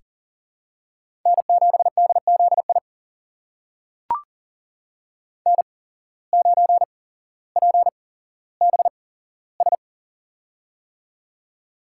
Morse code, heard as a keyed tone, about 38 wpm, spaced as for 17 wpm, 700 Hz. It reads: N7DZI N9PBS
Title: USA Call Signs - 5 Characters - 8x Character Spacing - 40wpm